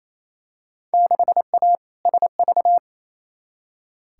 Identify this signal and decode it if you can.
Morse code, heard as a keyed tone, about 28 wpm, 700 Hz. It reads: 6A SV
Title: Field Day Exchanges Encore - 28wpm